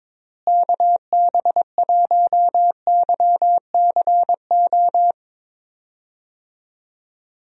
Morse code, heard as a keyed tone, about 22 wpm, 700 Hz. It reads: KB1YCO
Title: USA Call Signs - 6 Characters - Repeated 2x - 22wpm